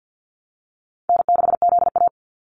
Morse code, heard as a keyed tone, about 50 wpm, 700 Hz. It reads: N6ZA
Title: NA SOTA Activator Callsigns 50wpm